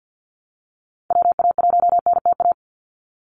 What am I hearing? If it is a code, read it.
Morse code, heard as a keyed tone, about 50 wpm, 700 Hz. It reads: WA1NTA
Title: Call Signs Repeated 2x - 50wpm